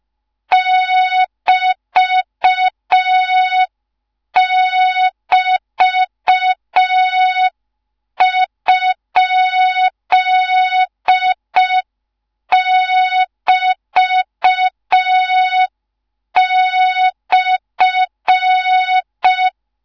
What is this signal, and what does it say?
Morse code, heard as a keyed tone, about 5 wpm, 3700 Hz. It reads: <BT><BT>?<BT>/